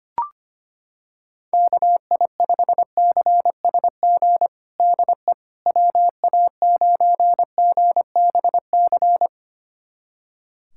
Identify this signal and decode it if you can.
Morse code, heard as a keyed tone, about 25 wpm, 700 Hz. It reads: KI5CSG DE WA9GBC
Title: USA Callsign DE Callsign - 6 Characters - 25wpm